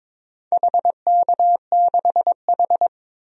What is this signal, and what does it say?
Morse code, heard as a keyed tone, about 22 wpm, 700 Hz. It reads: HK6H
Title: DXCC-100 Call Signs - 4 Characters - 22wpm